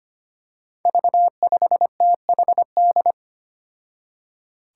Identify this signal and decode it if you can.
Morse code, heard as a keyed tone, about 25 wpm, 700 Hz. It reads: V5THD
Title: Five Challenging Letter-Number Combinations 25wpm